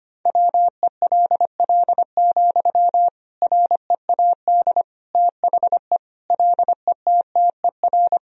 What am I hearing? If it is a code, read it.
Morse code, heard as a keyed tone, about 25 wpm, 700 Hz. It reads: WELL, READ THE LETTER
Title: Sentences from Top 300 Words and Abbreviations 25wpm